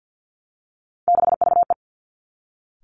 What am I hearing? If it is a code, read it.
Morse code, heard as a keyed tone, about 50 wpm, 700 Hz. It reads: 6VE
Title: Three Challenging Letter-Number Combinations 50wpm